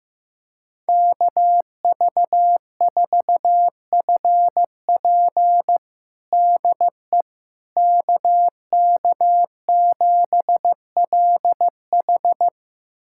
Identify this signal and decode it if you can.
Morse code, heard as a keyed tone, about 15 wpm, 700 Hz. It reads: KV4FP DE KK7LH
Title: USA Callsign DE Callsign - 5 Characters - 15wpm